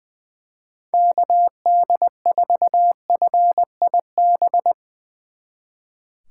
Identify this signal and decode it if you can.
Morse code, heard as a keyed tone, about 21 wpm, 700 Hz. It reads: KD4FIB
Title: Speed Racing - USA 2x3 Call Signs 20wpm